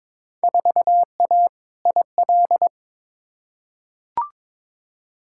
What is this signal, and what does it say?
Morse code, heard as a keyed tone, about 22 wpm, 700 Hz. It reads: 4A IL
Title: Field Day Exchanges - 22wpm